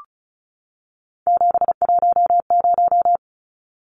Morse code, heard as a keyed tone, about 34 wpm, 700 Hz. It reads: 710
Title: Numbers - 3-Digit 35wpm